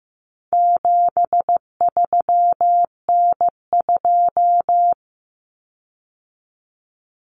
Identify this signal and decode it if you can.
Morse code, heard as a keyed tone, about 15 wpm, 700 Hz. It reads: 73N2
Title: Common Cut Numbers - 4-Digit 15wpm